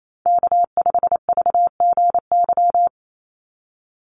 Morse code, heard as a keyed tone, about 28 wpm, 700 Hz. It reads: K5VGY
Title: USA Call Signs 28wpm